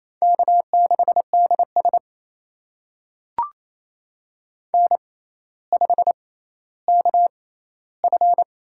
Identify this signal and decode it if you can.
Morse code, heard as a keyed tone, about 28 wpm, spaced as for 14 wpm, 700 Hz. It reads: K6DS N5KF
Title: USA Call Signs - 4 Characters - 6x Character Spacing - 28wpm